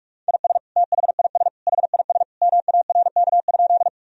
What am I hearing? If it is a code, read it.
Morse code, heard as a keyed tone, about 45 wpm, 700 Hz. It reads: IS THIS HIS MARK?
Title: Sentences from Top 1000 Words - Review Part 1 - 45wpm